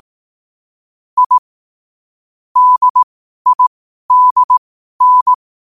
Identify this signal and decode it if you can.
Morse code, heard as a keyed tone, about 18 wpm, spaced as for 12 wpm, 1000 Hz. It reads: I DIDN
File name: qotdfiles / 12wpm